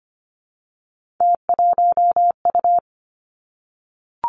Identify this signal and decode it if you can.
Morse code, heard as a keyed tone, about 25 wpm, 700 Hz. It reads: T1U
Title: Three Challenging Letter-Number Combinations 25wpm